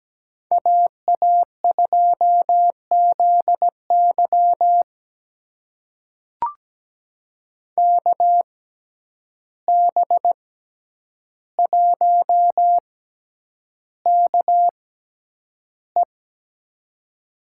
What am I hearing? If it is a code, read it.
Morse code, heard as a keyed tone, about 17 wpm, spaced as for 9 wpm, 700 Hz. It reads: AA2ZY KB1KE
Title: USA Call Signs - 5 Characters - 6x Character Spacing - 17wpm